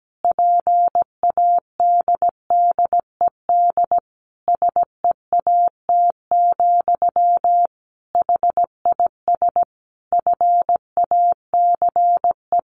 Morse code, heard as a keyed tone, about 17 wpm, 700 Hz. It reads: PADDED SEAT, HIS FACE
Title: A Sound Of Thunder 17wpm